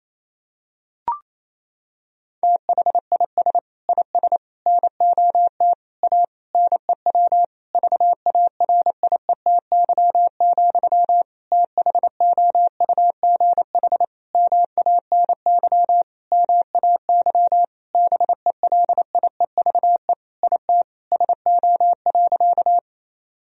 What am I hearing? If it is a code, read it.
Morse code, heard as a keyed tone, about 28 wpm, 700 Hz. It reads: THIS IS NOT A NEW VARIETY, THOUGH MANY MAY BELIEVE IT SO.
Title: Sentences from Top 1000 Words - Review Part 4 - 28wpm